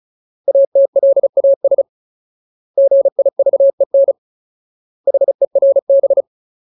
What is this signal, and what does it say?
Morse code, heard as a keyed tone, about 34 wpm, 550 Hz. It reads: ATLAS GIVEN HERB
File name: Ultimate Word List / 35WPM Ultimate